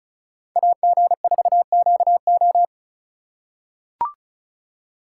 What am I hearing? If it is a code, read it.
Morse code, heard as a keyed tone, about 36 wpm, 700 Hz. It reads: AG4QO
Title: USA Call Signs - 5 Characters - 3x Character Spacing - 35wpm